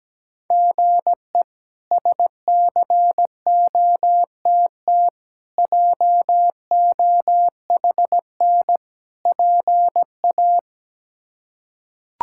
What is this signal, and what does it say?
Morse code, heard as a keyed tone, about 17 wpm, 700 Hz. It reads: GE SCOTT JOHN PA
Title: K1USN SST QSO Practice - 17wpm